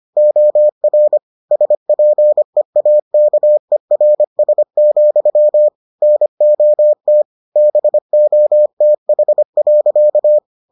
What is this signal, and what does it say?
Morse code, heard as a keyed tone, about 25 wpm, 600 Hz. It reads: OR SPEAKERS, NOT BOTH.